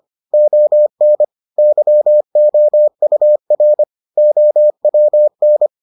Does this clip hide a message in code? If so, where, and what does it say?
Yes: Morse code from 0.3 to 5.7 s, heard as a keyed tone, about 25 wpm, 600 Hz. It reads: ON YOUR OWN